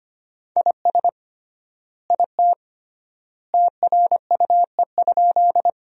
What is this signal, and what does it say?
Morse code, heard as a keyed tone, about 25 wpm, 700 Hz. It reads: IS IT TRUE?
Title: 3x Word-Spacing - Sentences from Top 400 Words - 25wpm